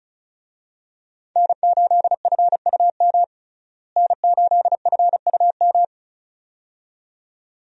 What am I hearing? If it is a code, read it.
Morse code, heard as a keyed tone, about 34 wpm, 700 Hz. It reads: N8FUM N8FUM
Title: USA Call Signs - 5 Characters - Repeated 2x - 35wpm